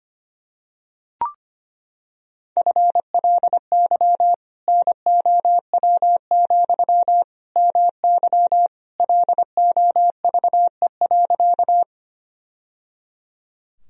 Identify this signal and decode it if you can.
Morse code, heard as a keyed tone, about 25 wpm, 700 Hz. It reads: FLY NOW, MY LOVE.
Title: Sentences from Top 1000 Words - Review Part 2 - 25wpm